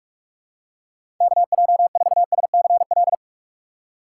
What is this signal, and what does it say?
Morse code, heard as a keyed tone, about 46 wpm, 700 Hz. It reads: KJ4SCL
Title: USA Call Signs - 6 Characters - 3x Character Spacing - 45wpm